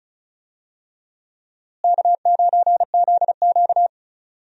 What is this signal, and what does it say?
Morse code, heard as a keyed tone, about 36 wpm, 700 Hz. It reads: K9ZQ
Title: USA Call Signs - 4 Characters - Repeated 2x - 35wpm